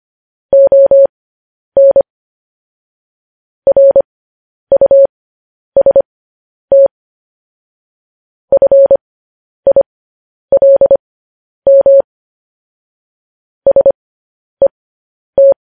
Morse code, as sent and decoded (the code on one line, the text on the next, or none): --- -. / .-. ..- ... - / ..-. .. .-.. -- / ... -
ON RUST FILM SET